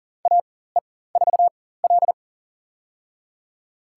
Morse code, heard as a keyed tone, about 39 wpm, spaced as for 19 wpm, 700 Hz. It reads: AE4L
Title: USA Call Signs - 4x Character Spacing - 40wpm